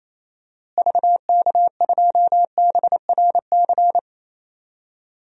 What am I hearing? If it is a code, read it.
Morse code, heard as a keyed tone, about 28 wpm, 700 Hz. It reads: VK2BRC
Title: DXCC-100 Call Signs - 6 Characters - Repeated 3x - 28wpm